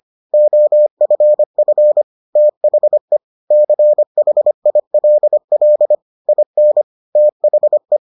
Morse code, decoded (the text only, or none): OFF THE CHILL IN THE